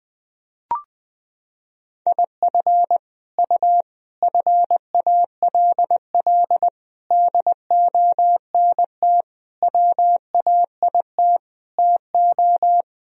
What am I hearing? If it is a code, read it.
Morse code, heard as a keyed tone, about 21 wpm, 700 Hz. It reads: IF U FALL DONT WAIT TO
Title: Sentences from Top 500 Words and Abbreviations 20wpm